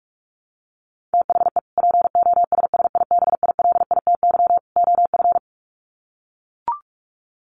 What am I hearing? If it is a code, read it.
Morse code, heard as a keyed tone, about 44 wpm, 700 Hz. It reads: THE POSSIBILITY OF
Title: Sets of 3 Words 45wpm